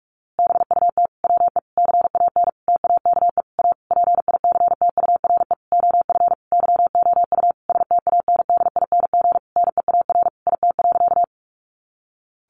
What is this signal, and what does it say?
Morse code, heard as a keyed tone, about 44 wpm, 700 Hz. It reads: BUT WE CAN TAKE A PICTURE OF YOU STANDING NEAR IT.
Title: A Sound Of Thunder 45wpm